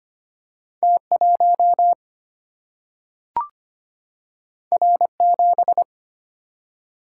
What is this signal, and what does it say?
Morse code, heard as a keyed tone, about 25 wpm, 700 Hz. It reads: T1 R7
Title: Single-Letter Single-Number 25wpm